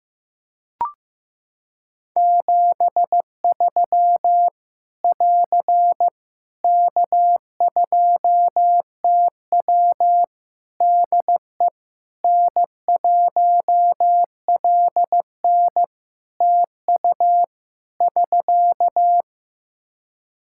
Morse code, heard as a keyed tone, about 15 wpm, 700 Hz. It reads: 73 <AR> K2TW DE N1LN TU <SK>
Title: Practice QSO - Exchange 4 - Ending - 15wpm